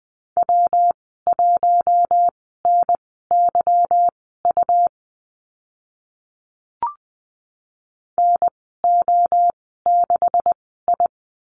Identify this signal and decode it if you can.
Morse code, heard as a keyed tone, about 20 wpm, spaced as for 15 wpm, 700 Hz. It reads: W1NYU NO6I
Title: 147 Lesson 24g - Letter y - Call Signs 20wpm